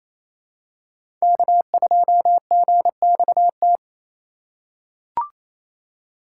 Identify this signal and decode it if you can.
Morse code, heard as a keyed tone, about 28 wpm, 700 Hz. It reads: K2GXT
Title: USA Call Signs - 2x Character Spacing - 28wpm